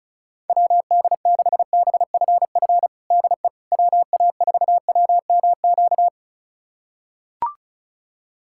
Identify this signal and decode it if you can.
Morse code, heard as a keyed tone, about 36 wpm, 700 Hz. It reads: WD6BFF DE WA4WMQ
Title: USA Callsign DE Callsign - 6 Characters - 35wpm